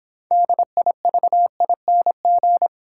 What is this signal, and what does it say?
Morse code, heard as a keyed tone, about 26 wpm, 700 Hz. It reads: DIVING